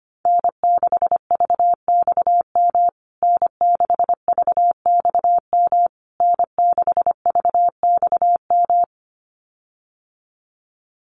Morse code, heard as a keyed tone, about 25 wpm, 700 Hz. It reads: N6VXM N6VXM N6VXM
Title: USA Call Signs Repeated 3x 25wpm